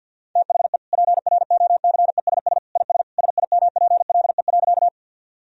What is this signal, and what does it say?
Morse code, heard as a keyed tone, about 46 wpm, 700 Hz. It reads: THE PROCESS IS SIMPLE.
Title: Sentences from Top 1000 Words - Review Part 3 - 50wpm